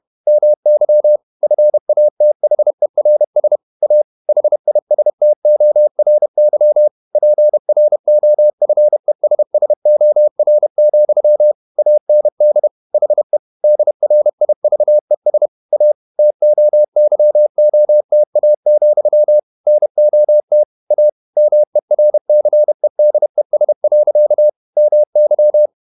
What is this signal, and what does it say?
Morse code, heard as a keyed tone, about 31 wpm, 600 Hz. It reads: MY FATHERS A HISTORY PROFESSOR, AND HE DRIVES A TOYOTA, NOT A MERCEDES. MY
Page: Never 31wpm | CW med Gnister